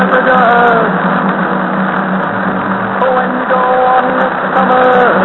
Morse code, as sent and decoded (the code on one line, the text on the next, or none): none